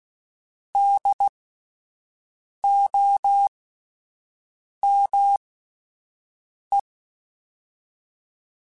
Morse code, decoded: DOME

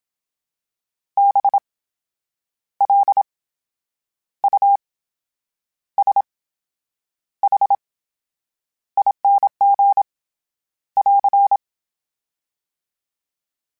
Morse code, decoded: BLUSHING <AR>

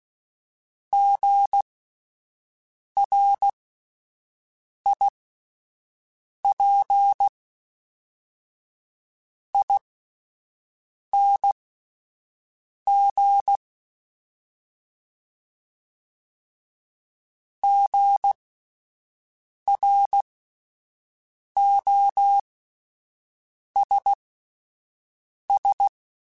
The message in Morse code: GRIPING GROSS

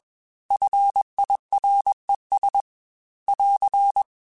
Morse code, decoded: FIRES <AR>

